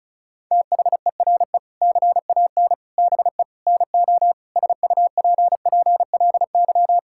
Morse code, heard as a keyed tone, about 34 wpm, 700 Hz. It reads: THERE CAN BE NO SUPPLY
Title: Sentences from Top 1000 Words - Review Part 5 - 35wpm